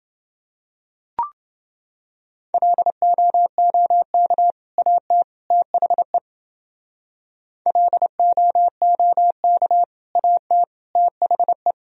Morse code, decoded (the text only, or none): LOOK AT THE LOOK AT THE